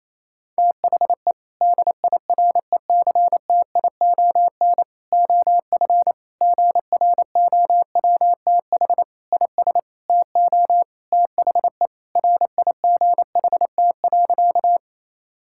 Morse code, heard as a keyed tone, about 28 wpm, 700 Hz. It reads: THE DIRECTION OF GROWTH IS TO THE RIGHT.
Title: Sentences from Top 1000 Words - Review Part 5 - 28wpm